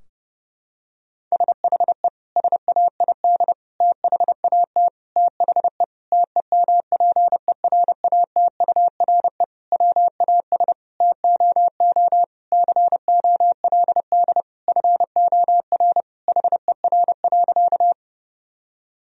Morse code, heard as a keyed tone, about 30 wpm, 700 Hz. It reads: SHE SAID THAT THE TEMPERATURE WAS TOO COLD FOR HER.
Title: Sentences from Top 1000 Words - Review Part 4 - 30wpm